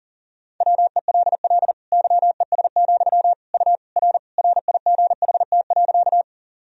Morse code, heard as a keyed tone, about 37 wpm, 700 Hz. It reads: WELL YES, U R RIGHT.
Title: Sentences from Top 500 Words and Abbreviations 40wpm